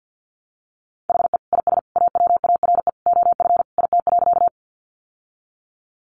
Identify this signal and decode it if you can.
Morse code, heard as a keyed tone, about 50 wpm, 700 Hz. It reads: HE IS AWARE OF IT.
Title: Sentences from Top 1200 Words 50wpm